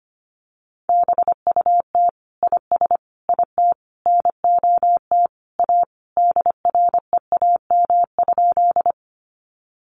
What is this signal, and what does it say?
Morse code, heard as a keyed tone, about 25 wpm, 700 Hz. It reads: BUT IS IT NOT A DREAM?
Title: Sentences from Top 1500 Words 25wpm